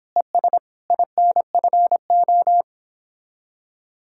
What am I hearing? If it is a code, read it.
Morse code, heard as a keyed tone, about 29 wpm, 700 Hz. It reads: ES INFO